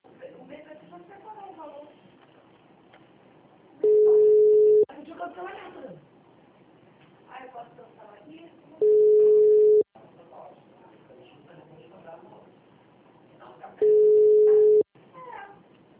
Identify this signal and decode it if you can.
Ringback tone (Europe, 425 Hz), with speech.